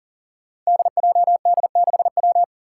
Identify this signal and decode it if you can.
Morse code, heard as a keyed tone, about 40 wpm, 700 Hz. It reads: DJD6W
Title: Five Challenging Letter-Number Combinations 40wpm